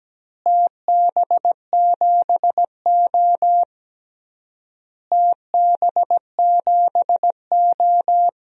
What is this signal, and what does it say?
Morse code, heard as a keyed tone, about 17 wpm, 700 Hz. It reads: TB7O TB7O
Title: DXCC-100 Call Signs - 4 Characters - Repeated 3x - 17wpm